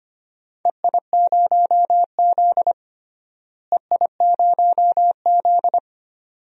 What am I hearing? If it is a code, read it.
Morse code, heard as a keyed tone, about 25 wpm, 700 Hz. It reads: EI0Z EI0Z
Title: DXCC-100 Call Signs - 4 Characters - Repeated 2x - 25wpm